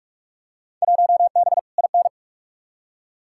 Morse code, heard as a keyed tone, about 45 wpm, 700 Hz. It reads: 1B IN
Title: Field Day Exchanges - 45wpm